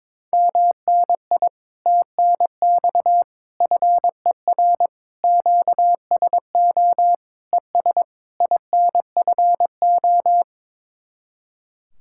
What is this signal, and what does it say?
Morse code, heard as a keyed tone, about 22 wpm, 700 Hz. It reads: MNI TNX FER QSO ES INFO
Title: Speed-Racing Common QSO Phrases - 22wpm